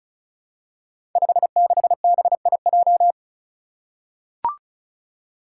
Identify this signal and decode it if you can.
Morse code, heard as a keyed tone, about 34 wpm, 700 Hz. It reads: 56BIJ
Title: Five Challenging Letter-Number Combinations 35wpm